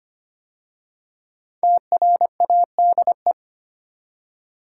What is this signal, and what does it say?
Morse code, heard as a keyed tone, about 25 wpm, 700 Hz. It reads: TRADE
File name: 5 Letter Words - First 3 Letters Given - 25wpm